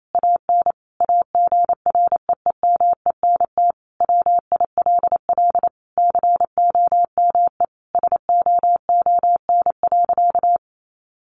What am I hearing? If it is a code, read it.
Morse code, heard as a keyed tone, about 28 wpm, 700 Hz. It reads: AN AGREEMENT WILL COME SOON.
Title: Sentences from Top 1500 Words 28wpm